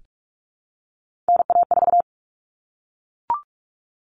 Morse code, heard as a keyed tone, about 44 wpm, 700 Hz. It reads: NA4
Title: All Cut Numbers - 3-Digit 45wpm